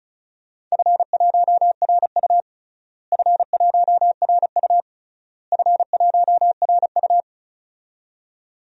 Morse code, heard as a keyed tone, about 34 wpm, 700 Hz. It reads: F1RU F1RU F1RU
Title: DXCC-100 Call Signs - 4 Characters - Repeated 3x - 35wpm